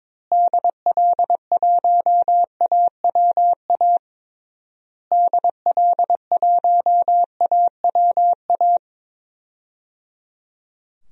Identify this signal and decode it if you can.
Morse code, heard as a keyed tone, about 22 wpm, 700 Hz. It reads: DL1AWA DL1AWA
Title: DXCC-100 Call Signs - 6 Characters - Repeated 3x - 22wpm